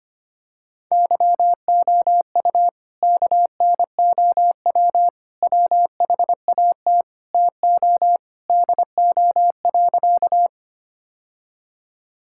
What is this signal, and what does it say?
Morse code, heard as a keyed tone, about 25 wpm, 700 Hz. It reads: YOU KNOW WHAT TO DO.
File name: practice / 25wpm / Sentences with 3-Word Phrases - 25wpm